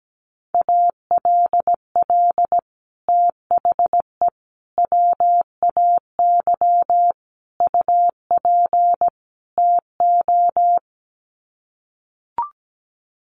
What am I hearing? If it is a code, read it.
Morse code, heard as a keyed tone, about 17 wpm, 700 Hz. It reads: ALL THE WAY UP TO
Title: Sets of 5 Words Encore 17wpm